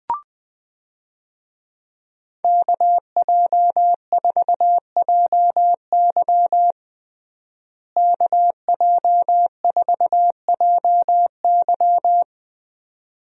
Morse code, heard as a keyed tone, about 20 wpm, 700 Hz. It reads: KJ4JY KJ4JY
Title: USA Call Signs - 5 Characters - Repeated 2x - 20wpm